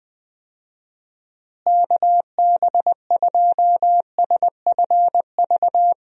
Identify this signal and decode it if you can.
Morse code, heard as a keyed tone, about 21 wpm, 700 Hz. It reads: KB2SFV
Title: USA Call Signs - 6 Characters - Repeated 2x - 20wpm